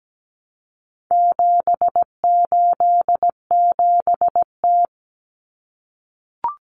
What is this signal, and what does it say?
Morse code, heard as a keyed tone, about 17 wpm, 700 Hz. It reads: 787T
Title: Common Cut Numbers - 4-Digit 17wpm